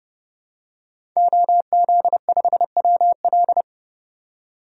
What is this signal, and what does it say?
Morse code, heard as a keyed tone, about 32 wpm, 700 Hz. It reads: OZ5WL
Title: DXCC-100 Call Signs - 5 Characters - 30wpm